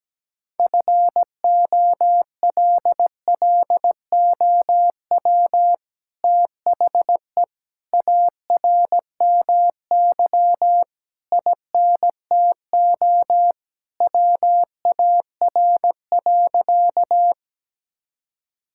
Morse code, heard as a keyed tone, about 17 wpm, 700 Hz. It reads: FOLLOW THE ARMY INTO WAR.
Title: Sentences from Top 1000 Words - Review Part 5 - 17wpm